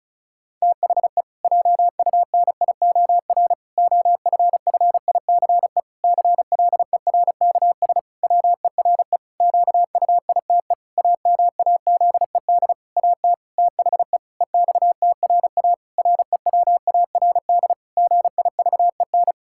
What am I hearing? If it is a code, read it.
Morse code, heard as a keyed tone, about 36 wpm, 700 Hz. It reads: THE JUNIOR OFFICE CLERKS WERE QUITE AMAZED AT THE EXTRA REWARD GIVEN
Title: Pangram Challenge v2 - 35wpm